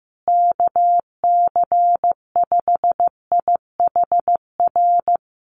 Morse code, heard as a keyed tone, about 15 wpm, 700 Hz. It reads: KC5IHR
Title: USA Call Signs Repeated 3x 15wpm